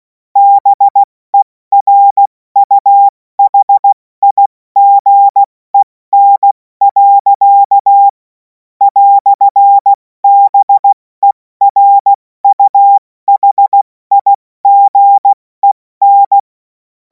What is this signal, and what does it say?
Morse code, heard as a keyed tone, about 16 wpm, spaced as for 14 wpm, 800 Hz. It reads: BERUHIGEN. "BERUHIGEN
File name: Morsekurs / Witze / Witz 03. Februar 2026 70 F80